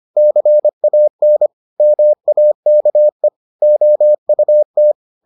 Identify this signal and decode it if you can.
Morse code, heard as a keyed tone, about 25 wpm, 600 Hz. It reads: CAN MAKE OUT